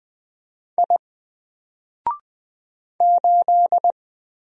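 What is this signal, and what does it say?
Morse code, heard as a keyed tone, about 20 wpm, 700 Hz. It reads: I 8
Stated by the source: Single letter and number Morse drill at 20 wpm, from N4LCD.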